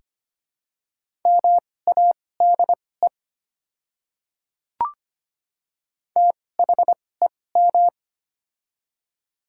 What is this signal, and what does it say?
Morse code, heard as a keyed tone, about 25 wpm, spaced as for 18 wpm, 700 Hz. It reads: MADE THEM
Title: 2x Character Spacing - Top 200 Words - 25wpm